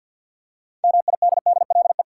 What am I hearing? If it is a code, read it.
Morse code, heard as a keyed tone, about 52 wpm, 700 Hz. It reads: MIDDLE
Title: Six Letter Words - 50wpm